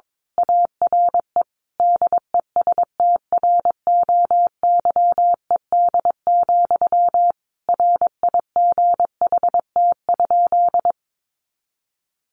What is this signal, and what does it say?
Morse code, heard as a keyed tone, about 22 wpm, 700 Hz. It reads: ARE DESTROYED, RIGHT?
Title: A Sound Of Thunder 22wpm